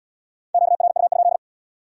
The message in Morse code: QRU?

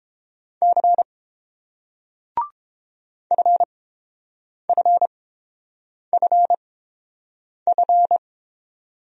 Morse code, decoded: C FFFF

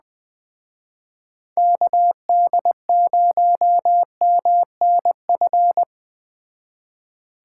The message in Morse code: KD0MNF

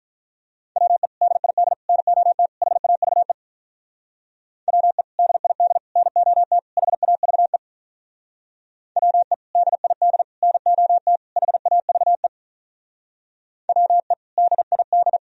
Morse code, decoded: WE DID NOT HAVE WE DID NOT HAVE WE DID NOT HAVE WE DID

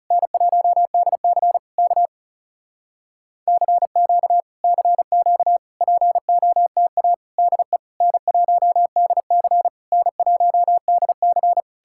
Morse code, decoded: N1DC K CQ CQ POTA DE N1DC N1DC

